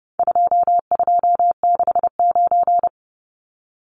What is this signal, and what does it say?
Morse code, heard as a keyed tone, about 29 wpm, 700 Hz. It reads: 2269
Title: Numbers - 4-Digit 30wpm